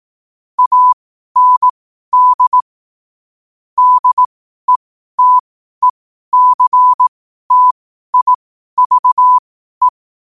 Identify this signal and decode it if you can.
Morse code, heard as a keyed tone, about 18 wpm, spaced as for 12 wpm, 1000 Hz. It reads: AND DETECTIVE
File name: qotdfiles / 12wpm